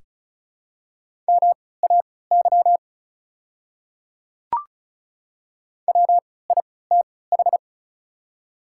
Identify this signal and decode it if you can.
Morse code, heard as a keyed tone, about 35 wpm, spaced as for 20 wpm, 700 Hz. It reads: MAY WITH
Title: 3x Character Spacing - Top 500 Words - 35wpm